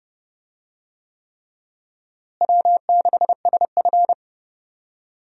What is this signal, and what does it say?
Morse code, heard as a keyed tone, about 32 wpm, 700 Hz. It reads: W6SF
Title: USA Call Signs - 4 Characters - Repeated 3x - 30wpm